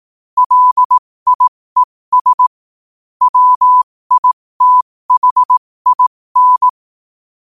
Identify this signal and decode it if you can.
Morse code, heard as a keyed tone, about 18 wpm, spaced as for 15 wpm, 1000 Hz. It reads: LIES WITHIN
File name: qotdfiles / 15wpm